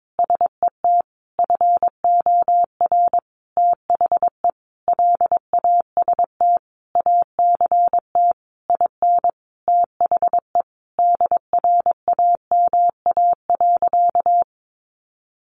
Morse code, heard as a keyed tone, about 22 wpm, 700 Hz. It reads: SET FOR THE LAST ACT IN THE DRAMA.
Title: Sentences from Top 1500 Words 22wpm